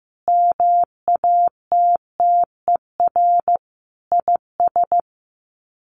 Morse code, heard as a keyed tone, about 15 wpm, 700 Hz. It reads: MATTER IS
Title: Sets of 5 Words 15wpm